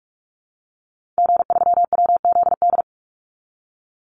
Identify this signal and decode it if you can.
Morse code, heard as a keyed tone, about 44 wpm, 700 Hz. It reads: G3WZD
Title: Call Signs Repeated 2x - 45wpm